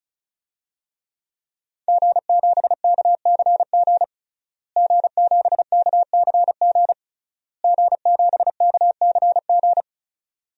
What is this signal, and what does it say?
Morse code, heard as a keyed tone, about 36 wpm, 700 Hz. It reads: G7KCG G7KCG G7KCG
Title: DXCC-100 Call Signs - 5 Characters - Repeated 3x - 35wpm